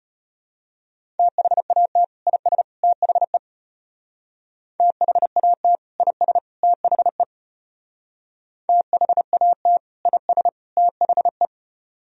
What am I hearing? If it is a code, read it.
Morse code, heard as a keyed tone, about 30 wpm, 700 Hz. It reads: THAT IS THE THAT IS THE THAT IS THE
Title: Speed-Racing - Sets of 3 Words 25wpm